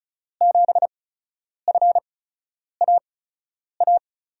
Morse code, heard as a keyed tone, about 34 wpm, spaced as for 10 wpm, 700 Hz. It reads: 7FAA
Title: USA Call Signs - 5 Characters - 8x Character Spacing - 35wpm